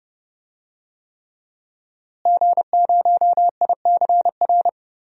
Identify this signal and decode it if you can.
Morse code, heard as a keyed tone, about 30 wpm, 700 Hz. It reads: G0ICR
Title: DXCC-100 Call Signs - 5 Characters - Repeated 3x - 30wpm